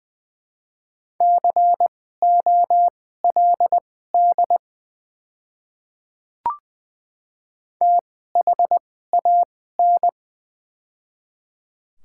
Morse code, heard as a keyed tone, about 20 wpm, spaced as for 15 wpm, 700 Hz. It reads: COLD THAN
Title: 2x Character Spacing - Top 500 Words - 20wpm